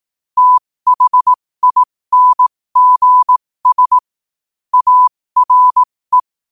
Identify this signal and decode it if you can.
Morse code, heard as a keyed tone, about 18 wpm, spaced as for 15 wpm, 1000 Hz. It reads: THINGS ARE
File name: qotdfiles / 15wpm